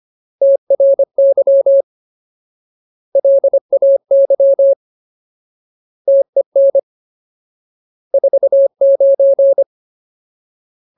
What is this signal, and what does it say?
Morse code, heard as a keyed tone, about 25 wpm, 550 Hz. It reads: TRY LAY TEN 49
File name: Ultimate Word List / 25WPM Ultimate